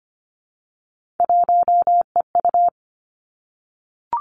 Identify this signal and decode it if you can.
Morse code, heard as a keyed tone, about 25 wpm, 700 Hz. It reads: 1EU
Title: Three Challenging Letter-Number Combinations 25wpm